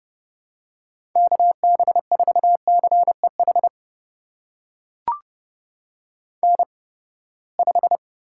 Morse code, heard as a keyed tone, about 32 wpm, spaced as for 21 wpm, 700 Hz. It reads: KB4CEH N5
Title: USA Call Signs - 8x Character Spacing - 30wpm